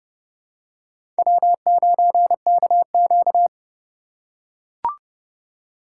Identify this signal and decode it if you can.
Morse code, heard as a keyed tone, about 31 wpm, 700 Hz. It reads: W9KQ
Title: USA Call Signs - 4 Characters - 2x Character Spacing - 30wpm